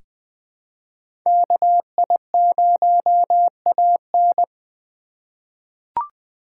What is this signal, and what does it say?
Morse code, heard as a keyed tone, about 20 wpm, 700 Hz. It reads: KI0AN
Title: USA Call Signs - 2x Character Spacing - 20wpm